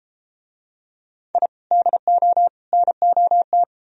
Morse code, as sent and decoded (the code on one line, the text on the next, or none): .. / -.. --- / -. --- -
I DO NOT